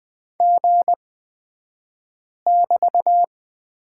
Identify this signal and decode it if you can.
Morse code, heard as a keyed tone, about 20 wpm, spaced as for 5 wpm, 700 Hz. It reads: G<BT>
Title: 223 Lesson 36d - BT - Two Characters 20wpm